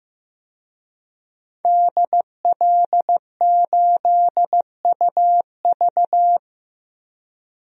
Morse code, heard as a keyed tone, about 15 wpm, 700 Hz. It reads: DL8UV
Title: DXCC-100 Call Signs - 5 Characters - Repeated 2x - 15wpm